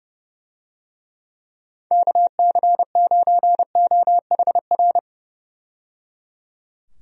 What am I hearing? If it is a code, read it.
Morse code, heard as a keyed tone, about 30 wpm, 700 Hz. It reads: KC9OHR
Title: USA Call Signs - 6 Characters - 30wpm